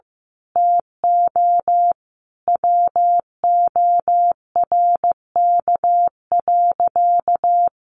Morse code, heard as a keyed tone, about 15 wpm, 700 Hz. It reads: TO WORK.
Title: Sentences from Top 1400 Words 15wpm